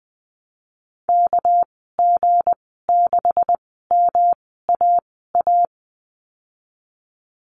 Morse code, heard as a keyed tone, about 19 wpm, spaced as for 14 wpm, 700 Hz. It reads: KG6MAA
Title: 219 Lesson 35g - Number 0 - Call Signs 20wpm